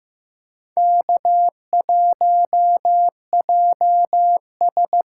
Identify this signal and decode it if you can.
Morse code, heard as a keyed tone, about 15 wpm, 700 Hz. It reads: K1JS
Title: USA Call Signs - 8x Character Spacing - 15wpm